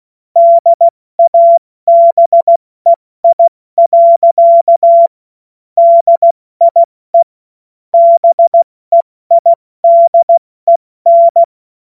Morse code, heard as a keyed tone, about 16 wpm, 680 Hz. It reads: DABEI. DIE BEIDEN